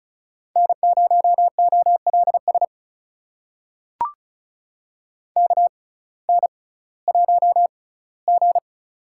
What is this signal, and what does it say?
Morse code, heard as a keyed tone, about 36 wpm, spaced as for 19 wpm, 700 Hz. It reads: N0OLS KN1G
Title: USA Call Signs - 6x Character Spacing - 35wpm